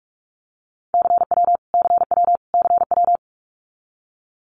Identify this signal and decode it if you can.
Morse code, heard as a keyed tone, about 44 wpm, 700 Hz. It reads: CW CW CW
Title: Abbreviations Repeated 3x - 45wpm